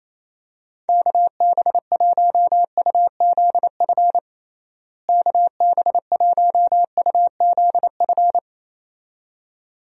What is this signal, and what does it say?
Morse code, heard as a keyed tone, about 28 wpm, 700 Hz. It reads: KB1UZF KB1UZF
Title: USA Call Signs - 6 Characters - Repeated 2x - 28wpm